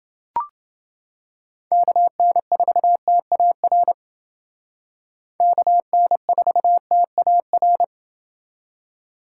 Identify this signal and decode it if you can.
Morse code, heard as a keyed tone, about 29 wpm, 700 Hz. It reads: KN4TAR KN4TAR